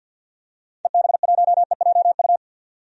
Morse code, heard as a keyed tone, about 52 wpm, 700 Hz. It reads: EB1EJU
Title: DXCC-100 Call Signs - 6 Characters - 50wpm